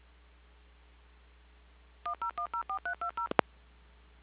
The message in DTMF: 10104320